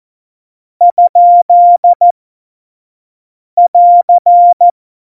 Morse code, heard as a keyed tone, about 14 wpm, spaced as for 5 wpm, 700 Hz. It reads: ?<AR>